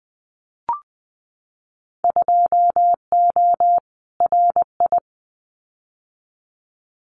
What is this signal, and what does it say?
Morse code, heard as a keyed tone, about 20 wpm, 700 Hz. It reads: 2O RI
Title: Winter Field Day Exchanges 20wpm